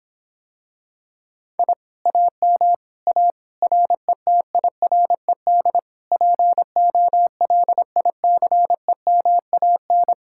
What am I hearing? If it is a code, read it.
Morse code, heard as a keyed tone, about 26 wpm, 700 Hz. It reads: I AM A RETIRED POLICEMAN